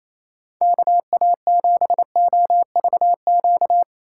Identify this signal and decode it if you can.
Morse code, heard as a keyed tone, about 28 wpm, 700 Hz. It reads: KA7OVQ
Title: USA Call Signs - 6 Characters - 4x Character Spacing - 28wpm